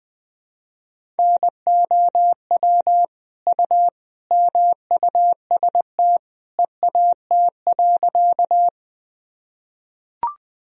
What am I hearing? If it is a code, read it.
Morse code, heard as a keyed tone, about 20 wpm, 700 Hz. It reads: NOW U MUST EAT.
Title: Sentences from Top 300 Words and Abbreviations 20wpm